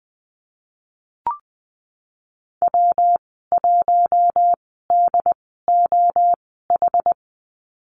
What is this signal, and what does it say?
Morse code, heard as a keyed tone, about 20 wpm, spaced as for 15 wpm, 700 Hz. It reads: W1DOH
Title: 167 Lesson 27g - Number 7 - Call Signs 20wpm